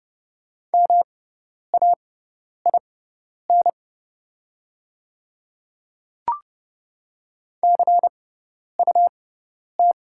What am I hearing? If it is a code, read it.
Morse code, heard as a keyed tone, about 30 wpm, spaced as for 10 wpm, 700 Hz. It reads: MAIN CUT
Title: 6x Character Spacing - Top 300 Words - 30wpm